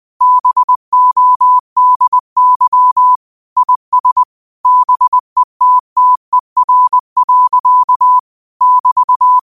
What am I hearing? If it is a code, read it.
Morse code, heard as a keyed tone, about 20 wpm, 1000 Hz. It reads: BODY IS BETTER. <BT>
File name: qotdfiles / 20wpm